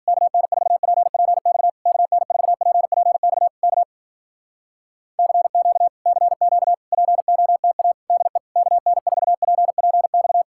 Morse code, heard as a keyed tone, about 52 wpm, 700 Hz. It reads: KN4PPX KN4PPX K CQ CQ POTA DE KN4PPX